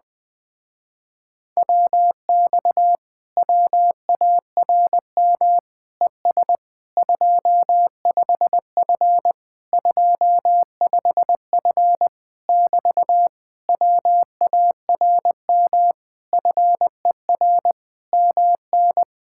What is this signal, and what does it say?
Morse code, heard as a keyed tone, about 20 wpm, 700 Hz. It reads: WX WARM ES 25F 25F <BT> WARM FER MN
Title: Speed-Racing Common QSO Phrases - 17wpm